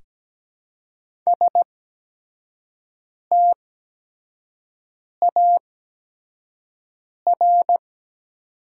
Morse code, heard as a keyed tone, about 17 wpm, spaced as for 5 wpm, 700 Hz. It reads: STAR
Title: 8x Character Spacing - Top 500 Words - 17wpm